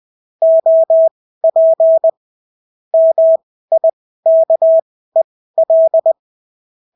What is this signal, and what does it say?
Morse code, heard as a keyed tone, about 20 wpm, spaced as for 14 wpm, 650 Hz. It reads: OP MIKEL
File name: CW QSO libroCW4 20-10ppm